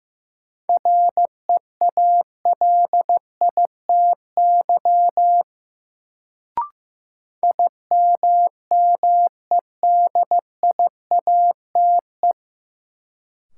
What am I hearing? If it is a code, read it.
Morse code, heard as a keyed tone, about 16 wpm, 700 Hz. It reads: REALITY IMMEDIATE
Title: Sentences from Top 1200 Words - New Word Drill - 15wpm